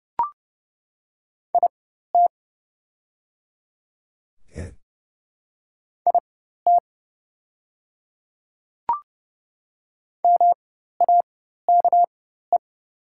Morse code, heard as a keyed tone, about 30 wpm, spaced as for 14 wpm, 690 Hz. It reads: IT IT MAKE
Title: 4x Character Spacing - Top 100 Words - 30wpm